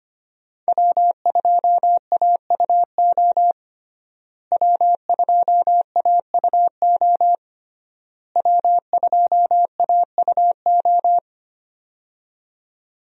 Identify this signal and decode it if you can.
Morse code, heard as a keyed tone, about 25 wpm, 700 Hz. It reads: W2AUO W2AUO W2AUO
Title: USA Call Signs - 5 Characters - Repeated 3x - 25wpm